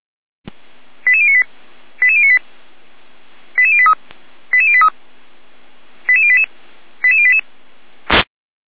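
Five-tone selective call (ZVEI), heard as quick runs of five tones, tones of about 70 ms. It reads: 88098 88098 88083 88083 88088 88088
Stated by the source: ZVEI five-tone alert received by SDR.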